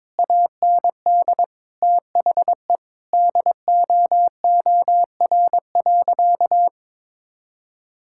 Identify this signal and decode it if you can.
Morse code, heard as a keyed tone, about 21 wpm, 700 Hz. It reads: AND THE DOOR.
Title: Sentences from Top 1600 Words 22wpm